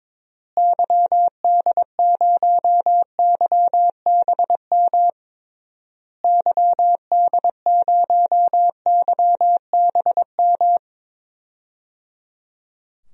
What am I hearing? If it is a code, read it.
Morse code, heard as a keyed tone, about 21 wpm, 700 Hz. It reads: YD0YBM YD0YBM
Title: DXCC-100 Call Signs - 6 Characters - Repeated 3x - 22wpm